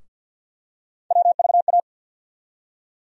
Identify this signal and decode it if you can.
Morse code, heard as a keyed tone, about 50 wpm, 700 Hz. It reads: WVA
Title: USA Call Sign Postfixes - 3 Letters 50wpm